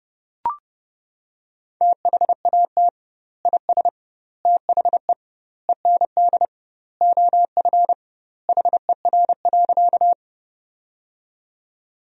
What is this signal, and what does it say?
Morse code, heard as a keyed tone, about 30 wpm, 700 Hz. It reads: THAT IS THE END OF HER.
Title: 2x Word-Spacing - Sentences from Top 200 Words - 30wpm